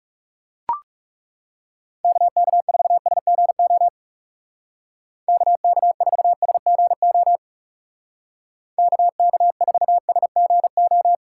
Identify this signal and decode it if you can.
Morse code, heard as a keyed tone, about 38 wpm, 700 Hz. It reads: KK4SGO KK4SGO KK4SGO